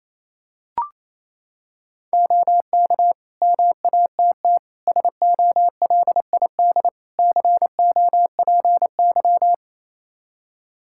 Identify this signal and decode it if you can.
Morse code, heard as a keyed tone, about 27 wpm, 700 Hz. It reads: OK MATT SOLID COPY
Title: Practice QSO - Exchange 3 - More Info - 28wpm